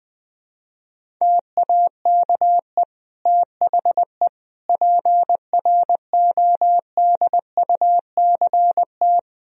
Morse code, heard as a keyed tone, about 20 wpm, 700 Hz. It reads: TAKE THE PRODUCT
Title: Sentences from Top 1000 Words - Review Part 2 - 20wpm